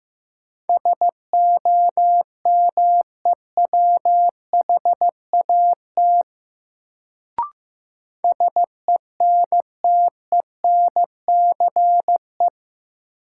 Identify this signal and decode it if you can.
Morse code, heard as a keyed tone, about 15 wpm, 700 Hz. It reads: SOMEWHAT SENTENCE
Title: Sentences from Top 1000 Words - Review Part 1 - New Word Drill - 15wpm